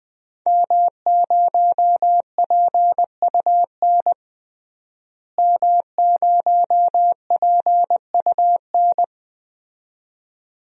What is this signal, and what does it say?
Morse code, heard as a keyed tone, about 20 wpm, 700 Hz. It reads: M0PUN M0PUN
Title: DXCC-100 Call Signs - 5 Characters - Repeated 2x - 20wpm